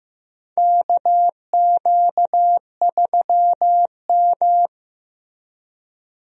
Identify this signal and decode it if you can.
Morse code, heard as a keyed tone, about 15 wpm, 700 Hz. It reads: KQ3M
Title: USA Call Signs - 4 Characters - Repeated 3x - 15wpm